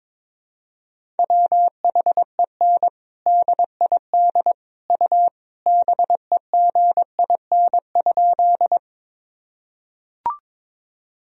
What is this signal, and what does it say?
Morse code, heard as a keyed tone, about 22 wpm, 700 Hz. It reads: WHEN DID U BEGIN?
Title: Sentences from Top 300 Words and Abbreviations 22wpm